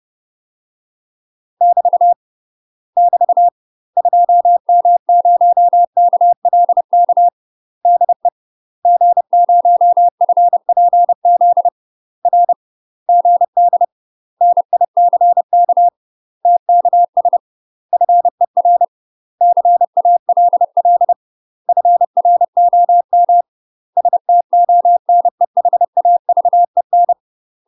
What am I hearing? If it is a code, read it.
Morse code, heard as a keyed tone, about 30 wpm, 700 Hz. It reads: <BT> <BT> 2M0KLK DE G0FPZ R GD NICK TKS FER CALL FROM STONEHAVEN
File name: UK MASTER QSOs TRAINING / 30WPM / UK MASTER QSOs 0036